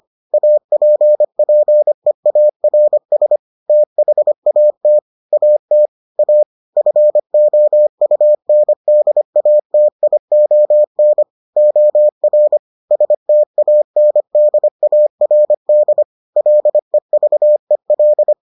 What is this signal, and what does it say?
Morse code, heard as a keyed tone, about 25 wpm, 600 Hz. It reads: APPEARS THAT AT A FOUNDATION OR STANDARD LEVEL